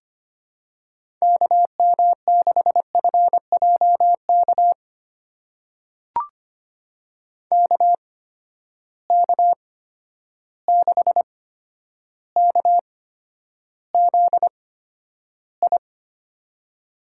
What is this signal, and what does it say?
Morse code, heard as a keyed tone, about 25 wpm, spaced as for 11 wpm, 700 Hz. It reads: KM6FJK KK6KZI
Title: USA Call Signs - 6 Characters - 8x Character Spacing - 25wpm